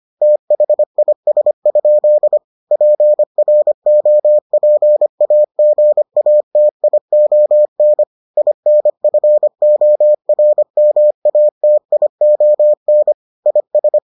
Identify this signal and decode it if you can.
Morse code, heard as a keyed tone, about 25 wpm, 600 Hz. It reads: THIS? PROPAGATION INFORMATION IS